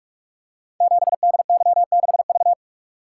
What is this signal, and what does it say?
Morse code, heard as a keyed tone, about 44 wpm, 690 Hz. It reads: 7DY6V
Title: Code Groups - 5 Letter:Numbers - 45wpm